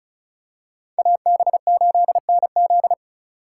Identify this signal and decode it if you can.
Morse code, heard as a keyed tone, about 36 wpm, 700 Hz. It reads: AB8NZ
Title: USA Call Signs - 5 Characters - 4x Character Spacing - 35wpm